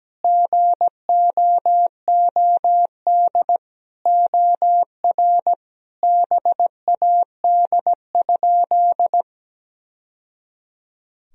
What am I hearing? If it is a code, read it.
Morse code, heard as a keyed tone, about 17 wpm, 700 Hz. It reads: GOOD OR BAD?
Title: Sentences from Top 1000 Words - Review Part 1 - 17wpm